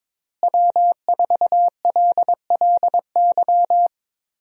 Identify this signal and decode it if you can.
Morse code, heard as a keyed tone, about 22 wpm, 700 Hz. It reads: W4LLY
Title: USA Call Signs - 5 Characters - Repeated 2x - 22wpm